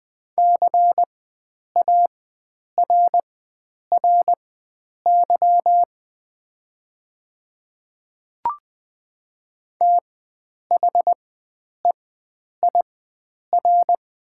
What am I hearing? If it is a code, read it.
Morse code, heard as a keyed tone, about 20 wpm, spaced as for 9 wpm, 700 Hz. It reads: CARRY THEIR
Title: 4x Character Spacing - Top 300 Words - 20wpm